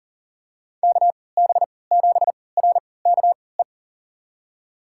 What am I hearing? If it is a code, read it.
Morse code, heard as a keyed tone, about 40 wpm, spaced as for 23 wpm, 700 Hz. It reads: KB7RKE
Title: USA Call Signs - 6 Characters - 3x Character Spacing - 40wpm